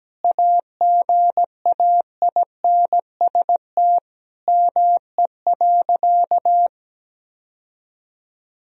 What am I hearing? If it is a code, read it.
Morse code, heard as a keyed tone, about 17 wpm, 700 Hz. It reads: AGAINST ME.
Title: Sentences from Top 1000 Words - Review Part 4 - 17wpm